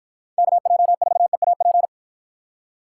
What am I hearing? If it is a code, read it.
Morse code, heard as a keyed tone, about 53 wpm, 700 Hz. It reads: KJ4EAP